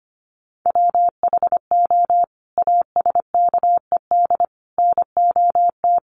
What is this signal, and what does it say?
Morse code, heard as a keyed tone, about 25 wpm, 700 Hz. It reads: WHO ASKED NOT
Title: Sets of 5 Words Encore2 25wpm